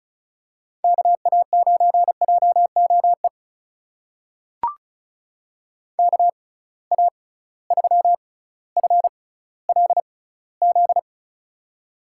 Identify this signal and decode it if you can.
Morse code, heard as a keyed tone, about 34 wpm, spaced as for 18 wpm, 700 Hz. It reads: KA9JOE KA3FLZ
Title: USA Call Signs - 6 Characters - 6x Character Spacing - 35wpm